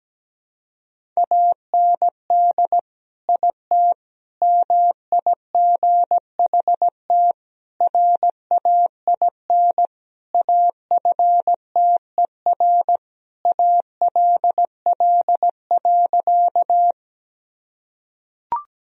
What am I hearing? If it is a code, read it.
Morse code, heard as a keyed tone, about 17 wpm, 700 Hz. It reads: AND IT MIGHT RAIN AFTER ALL.
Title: Sentences from Top 1000 Words - Review Part 1 - 17wpm